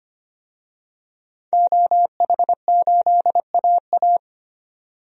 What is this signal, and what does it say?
Morse code, heard as a keyed tone, about 25 wpm, 700 Hz. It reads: OH8AA
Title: DXCC-100 Call Signs - 5 Characters - 25wpm